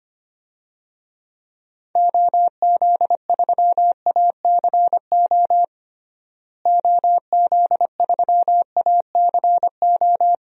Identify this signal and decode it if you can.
Morse code, heard as a keyed tone, about 25 wpm, 700 Hz. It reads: OZ3ACO OZ3ACO
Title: DXCC-100 Call Signs - 6 Characters - Repeated 3x - 25wpm